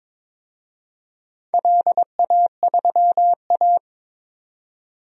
Morse code, heard as a keyed tone, about 21 wpm, 700 Hz. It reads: LA3A
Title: DXCC-100 Call Signs - 4 Characters - 22wpm